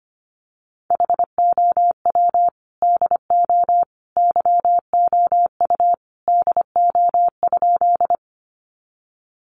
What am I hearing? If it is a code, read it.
Morse code, heard as a keyed tone, about 25 wpm, 700 Hz. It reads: HOW DO YOU DO?
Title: Sentences from Top 100 Words 25wpm